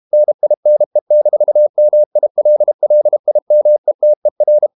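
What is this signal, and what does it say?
Morse code, heard as a keyed tone, about 32 wpm, 600 Hz. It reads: NINE-MILLIMETER
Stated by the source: Morse code practice at 32 wpm from CW med Gnister, reading Ken Follett's novel Never in English.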